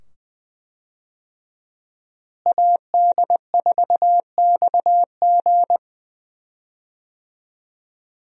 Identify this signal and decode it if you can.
Morse code, heard as a keyed tone, about 20 wpm, 700 Hz. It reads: AD4XG
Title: USA Call Signs - 5 Characters - Repeated 2x - 20wpm